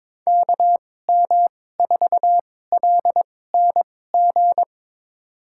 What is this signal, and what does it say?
Morse code, heard as a keyed tone, about 21 wpm, spaced as for 16 wpm, 700 Hz. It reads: KM4LNG
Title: USA Call Signs - 6 Characters - 2x Character Spacing - 22wpm